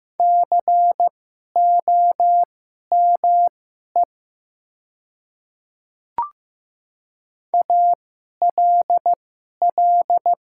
Morse code, heard as a keyed tone, about 15 wpm, spaced as for 11 wpm, 700 Hz. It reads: COME ALL
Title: 2x Character Spacing - Top 200 Words - 15wpm